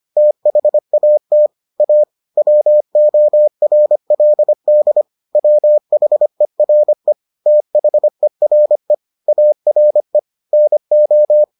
Morse code, heard as a keyed tone, about 25 wpm, 600 Hz. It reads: THAT A WORLD WHERE THERE ARE NO